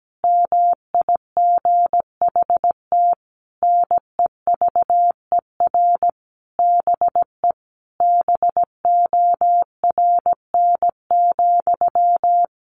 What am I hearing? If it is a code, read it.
Morse code, heard as a keyed tone, about 17 wpm, 700 Hz. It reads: MIGHT NEVER BE BORN,
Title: A Sound Of Thunder 17wpm